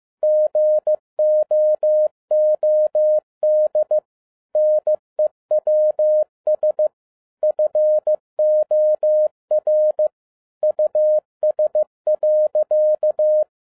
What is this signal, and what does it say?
Morse code, heard as a keyed tone, about 15 wpm, 610 Hz. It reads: GOOD NEWS FOR US.
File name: 2011-07-mp3 / n47 JOHN K3TN 15wpm